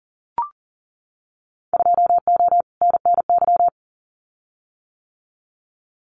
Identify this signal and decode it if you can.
Morse code, heard as a keyed tone, about 40 wpm, 700 Hz. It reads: 2O NNY
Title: Winter Field Day Exchanges 40wpm